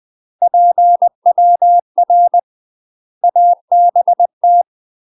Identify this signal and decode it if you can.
Morse code, heard as a keyed tone, about 20 wpm, 700 Hz. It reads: PWR ABT
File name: UK MASTER QSOs TRAINING / 20WPM / UK MASTER QSOs 0063